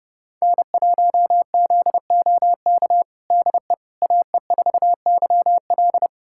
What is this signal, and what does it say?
Morse code, heard as a keyed tone, about 30 wpm, 700 Hz. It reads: N1ZOK DE AE4YL
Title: USA Callsign DE Callsign - 5 Characters - 30wpm